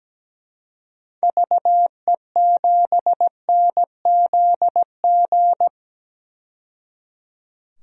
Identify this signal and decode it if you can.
Morse code, heard as a keyed tone, about 17 wpm, 700 Hz. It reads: VE7NZG
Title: DXCC-100 Call Signs - 6 Characters - 17wpm